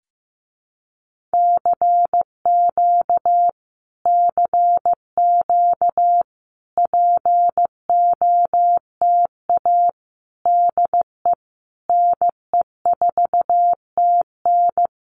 Morse code, heard as a keyed tone, about 15 wpm, 700 Hz. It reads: CQ CQ POTA DE NE4TN
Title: Common QSO Phrases Encore 15wpm